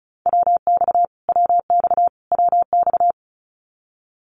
Morse code, heard as a keyed tone, about 34 wpm, 700 Hz. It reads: WX WX WX
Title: Abbreviations Repeated 3x - 35wpm